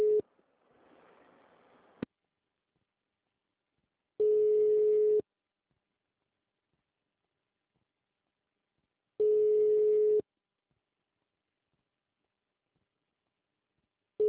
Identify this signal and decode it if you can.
Ringback tone (Europe, 425 Hz).
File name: out- -2010- - - .14